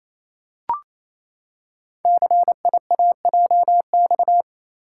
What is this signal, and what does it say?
Morse code, heard as a keyed tone, about 28 wpm, 700 Hz. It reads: CIAJX
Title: Code Groups - 5 Letters - 28wpm